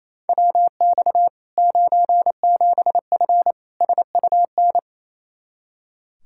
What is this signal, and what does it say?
Morse code, heard as a keyed tone, about 27 wpm, 700 Hz. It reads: WX 97F SUN
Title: Practice QSO - Exchange 2 - Info - 28wpm